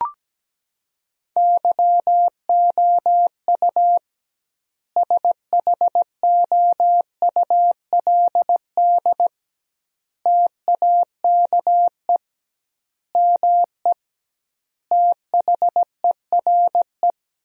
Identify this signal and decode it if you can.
Morse code, heard as a keyed tone, about 17 wpm, 700 Hz. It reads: YOU SHOULD TAKE ME THERE
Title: 2x Word-Spacing - Sentences from Top 300 Words - 17wpm